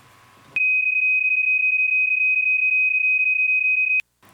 PSK31, about 31.25 baud.